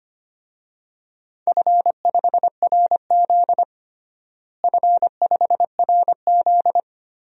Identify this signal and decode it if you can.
Morse code, heard as a keyed tone, about 25 wpm, 700 Hz. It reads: F5RZ F5RZ
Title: DXCC-100 Call Signs - 4 Characters - Repeated 3x - 25wpm